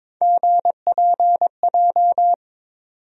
Morse code, heard as a keyed tone, about 22 wpm, 700 Hz. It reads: GPJ
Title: USA Call Sign Postfixes - 3 Letters 22wpm